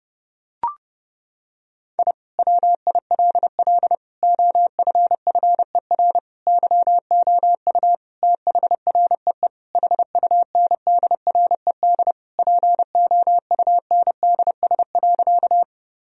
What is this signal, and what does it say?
Morse code, heard as a keyed tone, about 31 wpm, 700 Hz. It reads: I WILL OFFER YOU THREE HUNDRED POUNDS.
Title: Sentences from Top 1000 Words - Review Part 4 - 30wpm